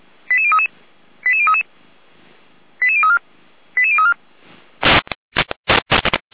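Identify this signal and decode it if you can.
Five-tone selective call (ZVEI), heard as quick runs of five tones, tones of about 80 ms. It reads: 88033 88033 88034 88034
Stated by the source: ZVEI five-tone selective call received on SDR.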